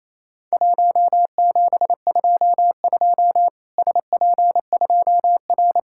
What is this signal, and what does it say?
Morse code, heard as a keyed tone, about 28 wpm, 700 Hz. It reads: 1722 SP2R
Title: Worked All Europe DX Contest - QTC Exchange - 28wpm